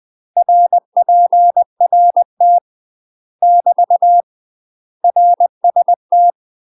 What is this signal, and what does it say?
Morse code, heard as a keyed tone, about 20 wpm, 700 Hz. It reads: RPRT <BT> RST
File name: UK MASTER QSOs TRAINING / 20WPM / UK MASTER QSOs 0010